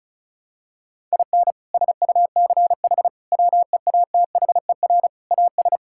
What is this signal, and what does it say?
Morse code, heard as a keyed tone, about 36 wpm, 700 Hz. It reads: IN SUCH WEATHER AS
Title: Sentences from Top 1000 Words - Review Part 5 - 35wpm